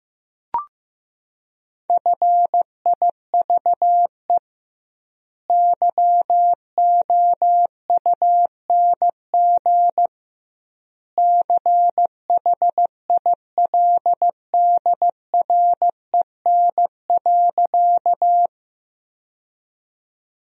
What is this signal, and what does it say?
Morse code, heard as a keyed tone, about 15 wpm, 700 Hz. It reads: FIVE YOUNG CHILDREN.
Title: 2x Word-Spacing - Sentences from Top 400 Words - 15wpm